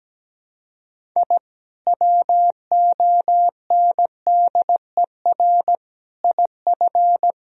Morse code, heard as a keyed tone, about 17 wpm, 700 Hz. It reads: I WONDER IF
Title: Sentences from Top 1000 Words - Review Part 3 - 17wpm